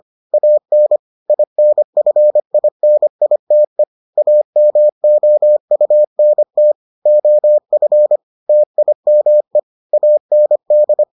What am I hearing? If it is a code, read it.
Morse code, heard as a keyed tone, about 25 wpm, 600 Hz. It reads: AN INFINITE AMOUNT OF TIME AND